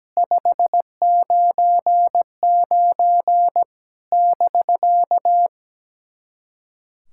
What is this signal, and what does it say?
Morse code, heard as a keyed tone, about 17 wpm, 700 Hz. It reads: 599 <BK>
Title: SOTA QSO Practice 17wpm